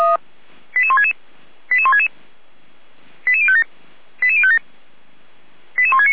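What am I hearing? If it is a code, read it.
Five-tone selective call (ZVEI), heard as quick runs of five tones, tones of about 70 ms. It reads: 88166 88166 88057 88057 88169
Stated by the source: ZVEI five-tone selective call, code 88191, received by SDR.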